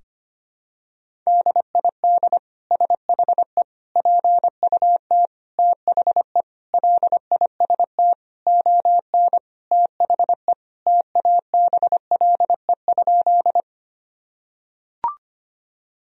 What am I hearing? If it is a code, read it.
Morse code, heard as a keyed tone, about 25 wpm, 700 Hz. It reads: DID SHE PUT THE LIST ON THE TABLE?
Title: Sentences from Top 1000 Words - Review Part 2 - 25wpm